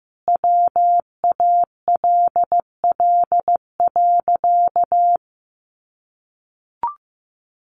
Morse code, heard as a keyed tone, about 14 wpm, 700 Hz. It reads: WALL.
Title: Sentences from Top 1500 Words 15wpm